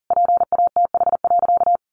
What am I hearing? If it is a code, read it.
Morse code, heard as a keyed tone, about 40 wpm, 700 Hz. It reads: PATH.
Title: A Sound Of Thunder 40wpm